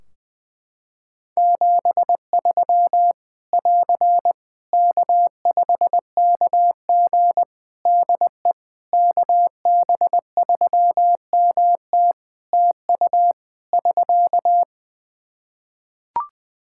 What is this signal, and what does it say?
Morse code, heard as a keyed tone, about 20 wpm, 700 Hz. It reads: 73 <AR> K5KG DE KB3MT TU <SK>
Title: K1USN SST QSO Practice 20wpm